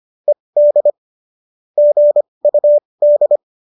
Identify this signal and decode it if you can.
Morse code, heard as a keyed tone, about 25 wpm, spaced as for 20 wpm, 570 Hz. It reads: ED GUD